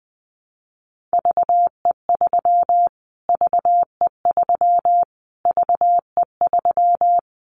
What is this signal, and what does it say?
Morse code, heard as a keyed tone, about 20 wpm, 700 Hz. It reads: VE3 VE3 VE3
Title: Callsign Prefixes for Top 25 DXCC - Repeated 3x - 20wpm